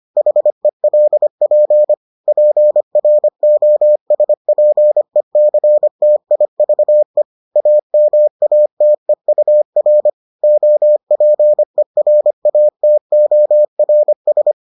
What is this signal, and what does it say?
Morse code, heard as a keyed tone, about 25 wpm, 600 Hz. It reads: HELP PROSPECTIVE AMATEUR OPERATORS